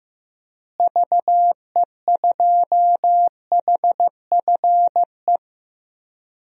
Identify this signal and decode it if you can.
Morse code, heard as a keyed tone, about 16 wpm, 700 Hz. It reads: VE2HFE
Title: DXCC-100 Call Signs - 6 Characters - Repeated 2x - 15wpm